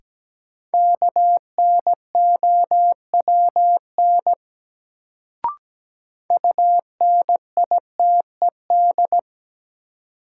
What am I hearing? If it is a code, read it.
Morse code, heard as a keyed tone, about 17 wpm, 700 Hz. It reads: KNOWN UNITED
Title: Sentences from Top 600 Words - New Word Drill - 17wpm